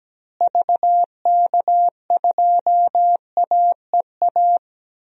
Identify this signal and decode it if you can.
Morse code, heard as a keyed tone, about 17 wpm, 700 Hz. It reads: VK2AEA
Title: DXCC-100 Call Signs - 6 Characters - Repeated 2x - 17wpm